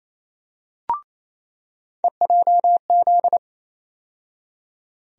Morse code, heard as a keyed tone, about 28 wpm, 700 Hz. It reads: EJZ
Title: USA Call Sign Postfixes - 3 Letters 28wpm